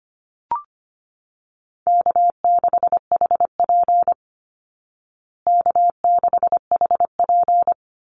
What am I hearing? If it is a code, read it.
Morse code, heard as a keyed tone, about 25 wpm, 700 Hz. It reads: K6HP K6HP
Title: Call Signs Repeated 2x - 25wpm